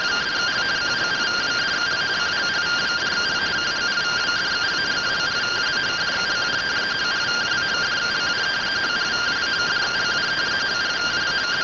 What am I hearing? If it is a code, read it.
Two-tone teleprinter signal, about 170 Hz shift.